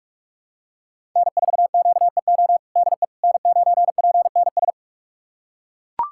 Morse code, heard as a keyed tone, about 42 wpm, 700 Hz. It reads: N4QEO DE N9PNS
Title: USA Callsign DE Callsign - 5 Characters - 45wpm